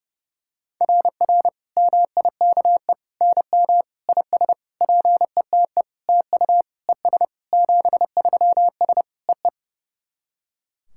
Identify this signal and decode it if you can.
Morse code, heard as a keyed tone, about 32 wpm, 700 Hz. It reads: RR MIKE NM IS PETE TU ES 73S EE
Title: SOTA QSO Practice 30wpm